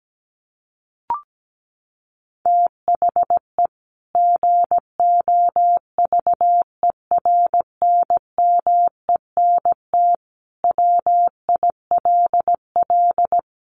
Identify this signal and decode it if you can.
Morse code, heard as a keyed tone, about 17 wpm, 700 Hz. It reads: THE GOVERNMENT WILL
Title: Sentences from Top 1500 Words 17wpm